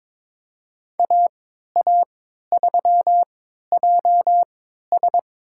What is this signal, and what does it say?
Morse code, heard as a keyed tone, about 22 wpm, spaced as for 13 wpm, 700 Hz. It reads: AA3JS
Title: USA Call Signs - 5 Characters - 3x Character Spacing - 22wpm